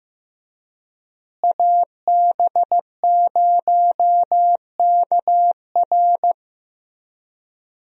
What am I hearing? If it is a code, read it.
Morse code, heard as a keyed tone, about 15 wpm, 700 Hz. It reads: AB0KR
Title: USA Call Signs - 5 Characters - Repeated 2x - 15wpm